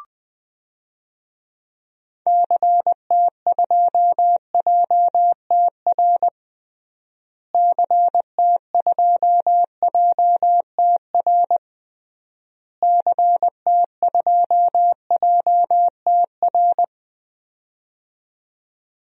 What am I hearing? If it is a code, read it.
Morse code, heard as a keyed tone, about 21 wpm, 700 Hz. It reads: CT2JTR CT2JTR CT2JTR
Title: DXCC-100 Call Signs - 6 Characters - Repeated 3x - 20wpm